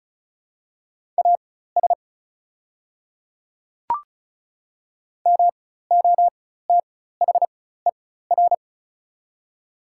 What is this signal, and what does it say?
Morse code, heard as a keyed tone, about 36 wpm, spaced as for 16 wpm, 690 Hz. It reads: AS MOTHER
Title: 4x Character Spacing - Top 200 Words - 35wpm